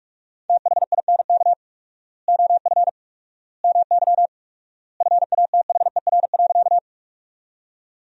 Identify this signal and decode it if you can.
Morse code, heard as a keyed tone, about 46 wpm, 700 Hz. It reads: THINK OF MY FATHER.
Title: 4x Word-Spacing - Sentences from Top 300 Words - 45wpm